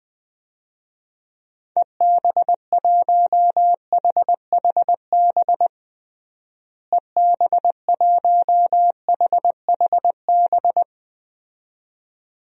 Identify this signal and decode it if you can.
Morse code, heard as a keyed tone, about 21 wpm, 700 Hz. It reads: EB1HHB EB1HHB
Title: DXCC-100 Call Signs - 6 Characters - Repeated 2x - 20wpm